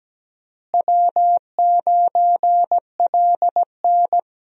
Morse code, heard as a keyed tone, about 17 wpm, 700 Hz. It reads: W9LN
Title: USA Call Signs - 4 Characters - Repeated 3x - 17wpm